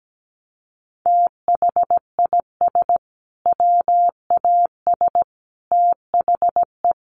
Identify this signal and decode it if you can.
Morse code, heard as a keyed tone, about 17 wpm, 700 Hz. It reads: THIS WAS THE
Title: Sentences from Top 900 Words 17wpm